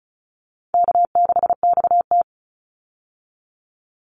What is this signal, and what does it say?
Morse code, heard as a keyed tone, about 34 wpm, 700 Hz. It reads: K6XT
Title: Easy Call Signs Repeated 2x - 35wpm